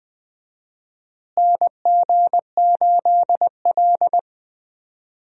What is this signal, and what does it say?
Morse code, heard as a keyed tone, about 20 wpm, 700 Hz. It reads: NG8L
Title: USA Call Signs - 4 Characters - Repeated 3x - 20wpm